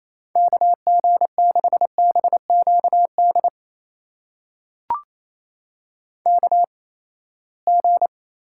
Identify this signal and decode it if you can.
Morse code, heard as a keyed tone, about 28 wpm, spaced as for 19 wpm, 700 Hz. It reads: KG6BQD KG
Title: USA Call Signs - 6 Characters - 8x Character Spacing - 28wpm